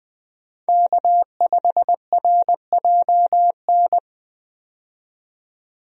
Morse code, heard as a keyed tone, about 20 wpm, 700 Hz. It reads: K5RJN
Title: USA Call Signs - 5 Characters - 20wpm